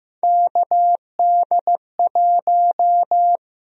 Morse code, heard as a keyed tone, about 15 wpm, 700 Hz. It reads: KD1
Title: USA Call Sign Prefixes - Repeated 2x - 15wpm